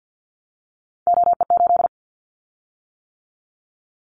Morse code, heard as a keyed tone, about 50 wpm, 700 Hz. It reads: OE8
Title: Callsign Prefixes for Top 100 DXCC - 50wpm